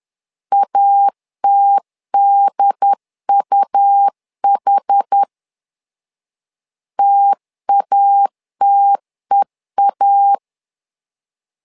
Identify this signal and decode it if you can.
Morse code, heard as a keyed tone, about 10 wpm, 800 Hz. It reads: ATDUH TATEA
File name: LEZIONI / 2 UHDN / 2 UHD 10wpm 1826